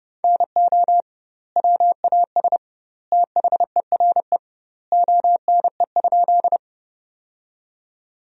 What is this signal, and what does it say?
Morse code, heard as a keyed tone, about 32 wpm, 700 Hz. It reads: NO WAS THERE ONE?
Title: 2x Word-Spacing - Sentences from Top 100 Words - 30wpm